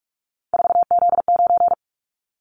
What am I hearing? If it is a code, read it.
Morse code, heard as a keyed tone, about 46 wpm, 700 Hz. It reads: 4Z9
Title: Callsign Prefixes for Top 50 DXCC - 45wpm